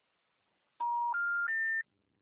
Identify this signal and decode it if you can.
Special information tone.
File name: out- -2011- - - .72942